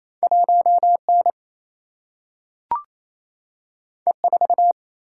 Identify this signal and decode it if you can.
Morse code, heard as a keyed tone, about 28 wpm, 700 Hz. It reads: 1 N E 4
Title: Single-Letter Single-Number 28wpm